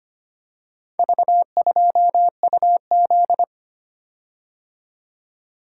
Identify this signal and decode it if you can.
Morse code, heard as a keyed tone, about 25 wpm, 700 Hz. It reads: V2UZ
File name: practice / 25wpm / Four Letter-Number Combinations - 25wpm